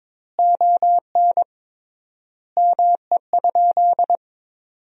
Morse code, heard as a keyed tone, about 22 wpm, 700 Hz. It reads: ON ME?
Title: 3x Word-Spacing - Sentences from Top 500 Words - 22wpm